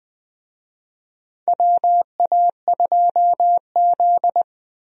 Morse code, heard as a keyed tone, about 20 wpm, 700 Hz. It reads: WA2Z
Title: USA Call Signs - 4 Characters - 20wpm